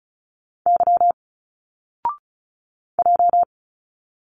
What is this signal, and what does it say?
Morse code, heard as a keyed tone, about 34 wpm, 700 Hz. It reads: Y J
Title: Single Letters 35wpm